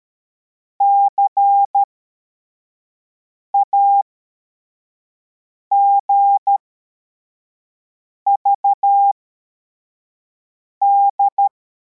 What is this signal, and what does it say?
Morse code, heard as a keyed tone, about 13 wpm, spaced as for 4 wpm, 800 Hz. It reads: CAGVD
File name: formation / cw / Text53